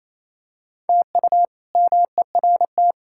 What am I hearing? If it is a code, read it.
Morse code, heard as a keyed tone, about 28 wpm, 700 Hz. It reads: TU MERT
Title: K1USN SST QSO Practice - 28wpm